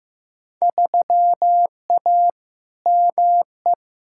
Morse code, heard as a keyed tone, about 15 wpm, 700 Hz. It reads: 3A ME
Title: Field Day Exchanges - 15wpm